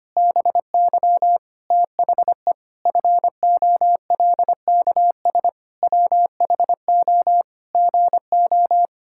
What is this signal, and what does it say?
Morse code, heard as a keyed tone, about 25 wpm, 700 Hz. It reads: BY THE FOLKS WHO GO
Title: Pangram Challenge v2 - 25wpm